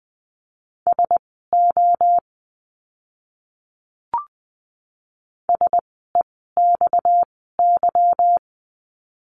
Morse code, heard as a keyed tone, about 20 wpm, spaced as for 15 wpm, 700 Hz. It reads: SO SEXY
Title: 230 Lesson 37f - Letter x - Words 20wpm